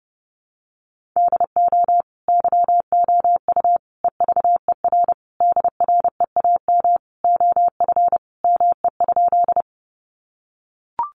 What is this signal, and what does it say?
Morse code, heard as a keyed tone, about 30 wpm, 700 Hz. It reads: DO YOU EVER DREAM OF ME?
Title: Sentences from Top 1500 Words 30wpm